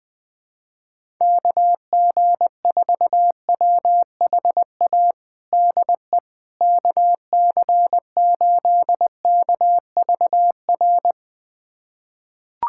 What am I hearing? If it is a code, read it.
Morse code, heard as a keyed tone, about 21 wpm, 700 Hz. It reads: KG4WHA DE KC8KVR
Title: USA Callsign DE Callsign - 6 Characters - 20wpm